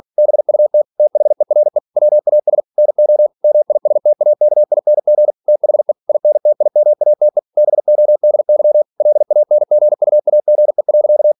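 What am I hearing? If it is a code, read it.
Morse code, heard as a keyed tone, about 46 wpm, 600 Hz. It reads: BUT THERE WAS NO MISTAKING THE INTIMATE BODY LANGUAGE.
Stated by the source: Practice text from an English novel sent at 47 WPM.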